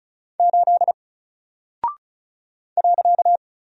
Morse code, heard as a keyed tone, about 36 wpm, 700 Hz. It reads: 8 .
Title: Single Character (Letter:Number:Punctuation:Prosign) Alt 35wpm